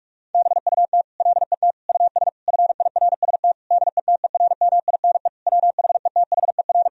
Morse code, heard as a keyed tone, about 41 wpm, 700 Hz. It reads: BUT LET US FIRST DETERMINE WHETHER
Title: Sentences from Top 1000 Words - Review Part 4 - 45wpm